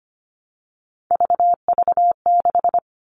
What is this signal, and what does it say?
Morse code, heard as a keyed tone, about 25 wpm, 700 Hz. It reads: VV6
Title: All Cut Numbers - 3-Digit 25wpm